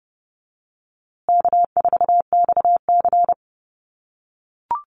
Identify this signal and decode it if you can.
Morse code, heard as a keyed tone, about 30 wpm, 700 Hz. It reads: K4XC
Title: USA Call Signs Repeated 3x 30wpm